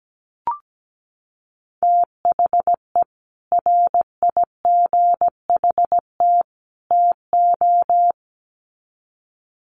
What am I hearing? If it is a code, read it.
Morse code, heard as a keyed tone, about 17 wpm, 700 Hz. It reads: THE RIGHT TO
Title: Sets of 3 Words 17wpm